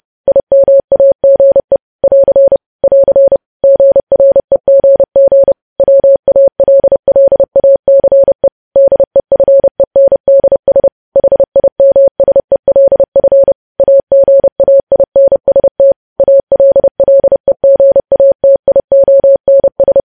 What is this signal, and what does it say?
Morse code, heard as a keyed tone, about 30 wpm, 550 Hz. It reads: IMAGE <AR> <AR> GREGG WALLACE DEFENDS HIMSELF AGAINST ALLEGATIONS
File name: podcast / News headlines-1x-30x30